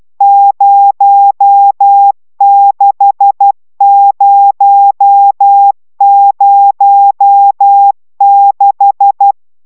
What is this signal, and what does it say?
Morse code, heard as a keyed tone, about 12 wpm, 800 Hz. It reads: 06006
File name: taviro tanulas / cw 69 20 10